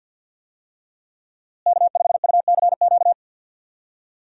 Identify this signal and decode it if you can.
Morse code, heard as a keyed tone, about 51 wpm, 700 Hz. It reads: K5UCQ
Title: USA Call Signs - 5 Characters - Repeated 3x - 50wpm